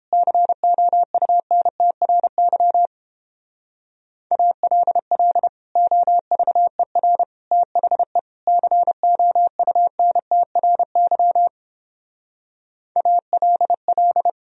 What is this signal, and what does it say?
Morse code, heard as a keyed tone, about 29 wpm, 700 Hz. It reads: COUNTRY ALL OVER THE COUNTRY ALL